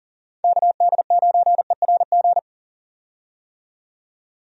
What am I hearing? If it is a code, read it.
Morse code, heard as a keyed tone, about 40 wpm, 700 Hz. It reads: KD9ERG
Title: USA Call Signs - 6 Characters - 40wpm